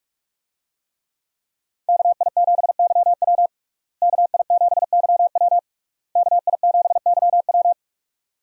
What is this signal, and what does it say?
Morse code, heard as a keyed tone, about 41 wpm, 700 Hz. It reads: KI7YW KI7YW KI7YW
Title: USA Call Signs - 5 Characters - Repeated 3x - 45wpm